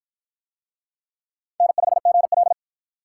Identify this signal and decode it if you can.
Morse code, heard as a keyed tone, about 47 wpm, 700 Hz. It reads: N5GL